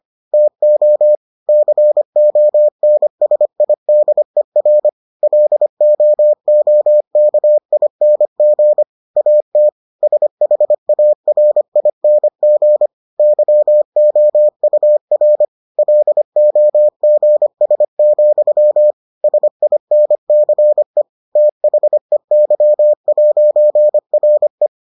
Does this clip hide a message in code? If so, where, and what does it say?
Yes: Morse code from 0.3 to 24.7 s, heard as a keyed tone, about 25 wpm, 600 Hz. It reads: TO CONSIDER LOOKING AT SHARING YOUR LOGS, SINCE THEY'RE